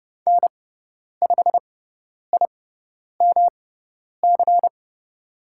Morse code, heard as a keyed tone, about 30 wpm, spaced as for 10 wpm, 700 Hz. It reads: N5IMC
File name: Lesson 18g - Letter m - Call Signs 30wpm